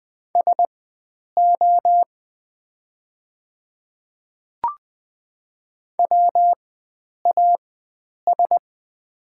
Morse code, heard as a keyed tone, about 20 wpm, spaced as for 9 wpm, 690 Hz. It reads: SO WAS
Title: 4x Character Spacing - Top 100 Words - 20wpm